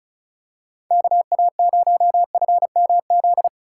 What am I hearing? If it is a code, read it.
Morse code, heard as a keyed tone, about 34 wpm, 690 Hz. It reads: KA0FMZ
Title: USA Call Signs - 6 Characters - 6x Character Spacing - 35wpm